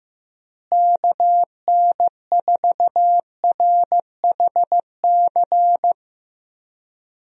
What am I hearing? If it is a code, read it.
Morse code, heard as a keyed tone, about 15 wpm, 700 Hz. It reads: KN4RHC
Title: USA Call Signs - 6 Characters - Repeated 3x - 15wpm